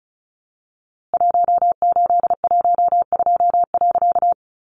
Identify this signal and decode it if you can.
Morse code, heard as a keyed tone, about 34 wpm, 700 Hz. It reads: 1812.
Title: A Sound Of Thunder 35wpm